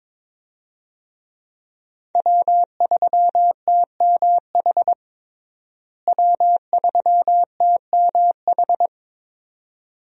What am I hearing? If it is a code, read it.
Morse code, heard as a keyed tone, about 22 wpm, 700 Hz. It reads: W3TMH W3TMH
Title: USA Call Signs - 5 Characters - Repeated 2x - 22wpm